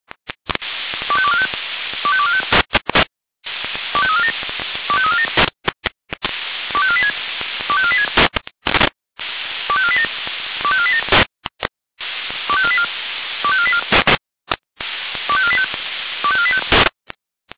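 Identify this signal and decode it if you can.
Five-tone selective call (ZVEI), heard as quick runs of five tones, tones of about 80 ms. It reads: 25245 25245 25347 25347 25486 25486 25487 25487 25484 25484 25485 25485
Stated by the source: ZVEI five-tone call 25492 received by SDR.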